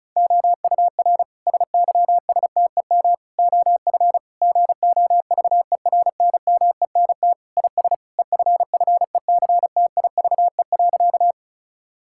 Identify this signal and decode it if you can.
Morse code, heard as a keyed tone, about 36 wpm, 700 Hz. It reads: OUR SYSTEM OF GOVERNMENT IS EFFECTIVE.
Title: Sentences from Top 1000 Words - Review Part 6 - 35wpm